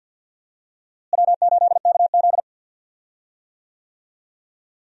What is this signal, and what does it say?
Morse code, heard as a keyed tone, about 52 wpm, 690 Hz. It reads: W8KZ
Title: USA Call Signs - 4 Characters - Repeated 2x - 50wpm